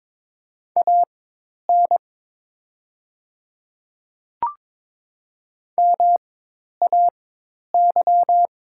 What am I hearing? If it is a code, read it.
Morse code, heard as a keyed tone, about 22 wpm, spaced as for 10 wpm, 700 Hz. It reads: AN MAY
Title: 4x Character Spacing - Top 100 Words - 22wpm